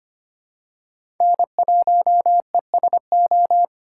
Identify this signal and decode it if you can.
Morse code, heard as a keyed tone, about 25 wpm, 700 Hz. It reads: N1ESO
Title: USA Call Signs - 4x Character Spacing - 25wpm